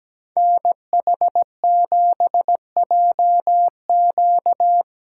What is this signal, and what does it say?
Morse code, heard as a keyed tone, about 17 wpm, 700 Hz. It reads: NH7JQ
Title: USA Call Signs - 8x Character Spacing - 17wpm